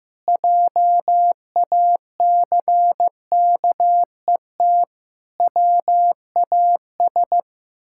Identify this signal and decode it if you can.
Morse code, heard as a keyed tone, about 15 wpm, 700 Hz. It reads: JACKET WAS
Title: Pangram Challenge v2 - 15wpm